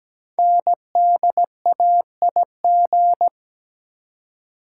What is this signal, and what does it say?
Morse code, heard as a keyed tone, about 17 wpm, 700 Hz. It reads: NDAIG
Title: Code Groups - 5 Letters - 17wpm